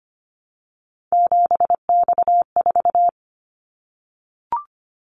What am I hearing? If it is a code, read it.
Morse code, heard as a keyed tone, about 25 wpm, 700 Hz. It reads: 7X4
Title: Callsign Prefixes for Top 100 DXCC - 25wpm